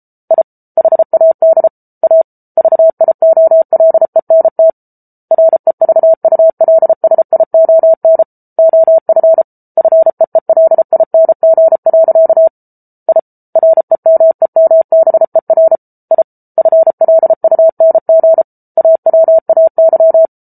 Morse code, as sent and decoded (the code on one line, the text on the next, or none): .. / .... .- -.. / .- / ...- .. --- .-.. -. - / .-. ...- ..- .-.. ... .. --- -. / --- ..-. / ..-. . .-.. .. -. --. .-.-.- / .. / .-. -- . -- -... .-. / .. / ..-. .-.. ..- -. --. / .- .-- .- -.--
I HAD A VIOLENT REVULSION OF FEELING. I REMEMBER I FLUNG AWAY